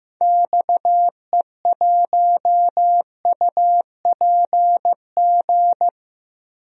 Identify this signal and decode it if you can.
Morse code, heard as a keyed tone, about 15 wpm, 700 Hz. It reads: XE1UPG
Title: DXCC-100 Call Signs - 6 Characters - 15wpm